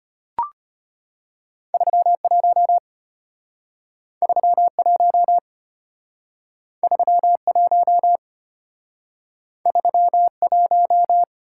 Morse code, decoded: 31 31 31 31

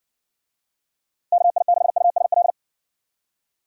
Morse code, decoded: KI6USL

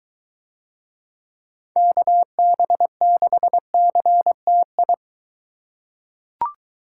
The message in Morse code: KB6CTI

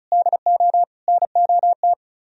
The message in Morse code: DO NOT

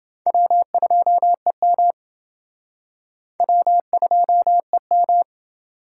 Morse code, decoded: W2EM W2EM

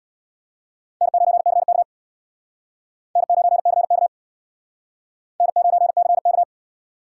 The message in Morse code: N9CK N9CK N9CK

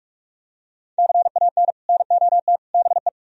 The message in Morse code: CAN NOT BE

